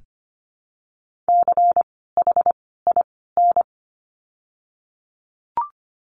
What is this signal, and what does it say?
Morse code, heard as a keyed tone, about 25 wpm, spaced as for 16 wpm, 700 Hz. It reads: CHIN
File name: course / 25:10wpm / 094 Lesson 16f - Letter c - Words 25wpm